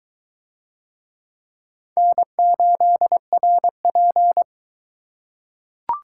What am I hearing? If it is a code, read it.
Morse code, heard as a keyed tone, about 20 wpm, 700 Hz. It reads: N8RP